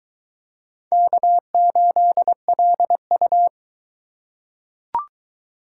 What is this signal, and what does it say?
Morse code, heard as a keyed tone, about 22 wpm, 700 Hz. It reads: K8LU